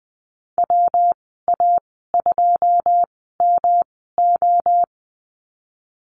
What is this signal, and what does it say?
Morse code, heard as a keyed tone, about 20 wpm, spaced as for 15 wpm, 700 Hz. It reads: WA2MO
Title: 140 Lesson 23g - Letter f - Call Signs 20wpm